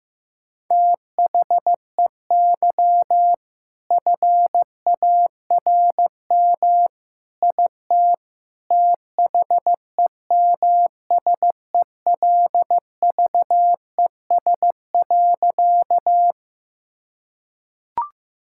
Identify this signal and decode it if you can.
Morse code, heard as a keyed tone, about 15 wpm, 690 Hz. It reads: THEY FARM IT THEMSELVES.
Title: Sentences from Top 1000 Words - Review Part 5 - 15wpm